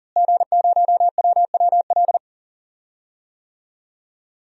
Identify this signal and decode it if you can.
Morse code, heard as a keyed tone, about 40 wpm, 700 Hz. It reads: G0WWL
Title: DXCC-100 Call Signs - 5 Characters - Repeated 3x - 40wpm